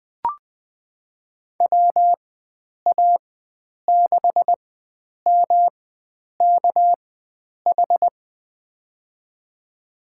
Morse code, heard as a keyed tone, about 20 wpm, spaced as for 9 wpm, 700 Hz. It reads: WA6MKH
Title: USA Call Signs - 4x Character Spacing - 20wpm